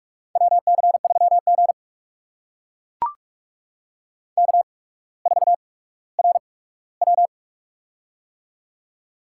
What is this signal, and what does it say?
Morse code, heard as a keyed tone, about 44 wpm, spaced as for 19 wpm, 700 Hz. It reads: WC3G K4RW
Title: USA Call Signs - 4 Characters - 8x Character Spacing - 45wpm